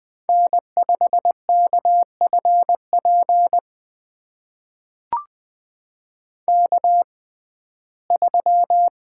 Morse code, heard as a keyed tone, about 20 wpm, spaced as for 14 wpm, 700 Hz. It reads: N5KFP K3
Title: USA Call Signs - 5 Characters - 6x Character Spacing - 20wpm